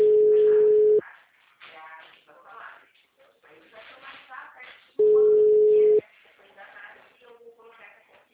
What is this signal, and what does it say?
Ringback tone (Europe, 425 Hz), with speech.